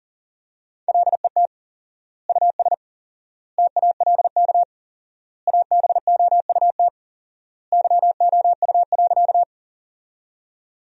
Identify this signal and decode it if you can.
Morse code, heard as a keyed tone, about 40 wpm, 700 Hz. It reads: LET US TALK ABOUT YOU.
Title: 4x Word-Spacing - Sentences from Top 400 Words - 40wpm